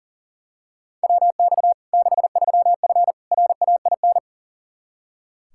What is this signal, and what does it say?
Morse code, heard as a keyed tone, about 40 wpm, 700 Hz. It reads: WX 63F RAIN
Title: Practice QSO - Exchange 2 - Info - 40wpm